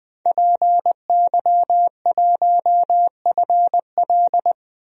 Morse code, heard as a keyed tone, about 20 wpm, 700 Hz. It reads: PY1FL
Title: DXCC-100 Call Signs - 5 Characters - Repeated 2x - 20wpm